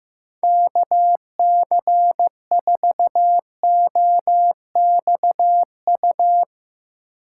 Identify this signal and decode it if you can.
Morse code, heard as a keyed tone, about 15 wpm, 700 Hz. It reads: KC4OXU
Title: USA Call Signs - 6 Characters - 3x Character Spacing - 15wpm